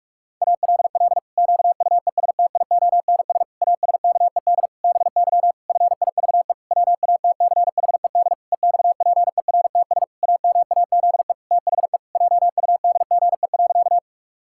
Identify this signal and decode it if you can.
Morse code, heard as a keyed tone, about 41 wpm, 700 Hz. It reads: ALL QUESTIONS ASKED BY FIVE WATCHED EXPERTS AMAZE THE JUDGE.
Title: Pangram Challenge v2 - 45wpm